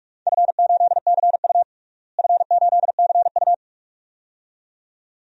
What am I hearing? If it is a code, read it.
Morse code, heard as a keyed tone, about 46 wpm, 700 Hz. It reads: F8CU F8CU
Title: DXCC-100 Call Signs - 4 Characters - Repeated 2x - 45wpm